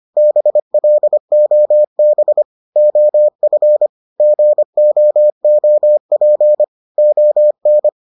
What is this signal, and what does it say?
Morse code, heard as a keyed tone, about 25 wpm, 600 Hz. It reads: BLOB OF GOOP ON